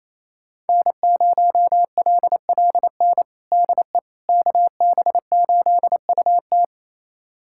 Morse code, heard as a keyed tone, about 27 wpm, 700 Hz. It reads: N0LLN DE KB8UT
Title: USA Callsign DE Callsign - 5 Characters - 28wpm